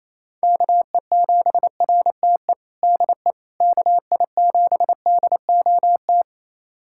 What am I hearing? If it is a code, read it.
Morse code, heard as a keyed tone, about 27 wpm, 700 Hz. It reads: KE7RTE DE KI7DOT
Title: USA Callsign DE Callsign - 6 Characters - 28wpm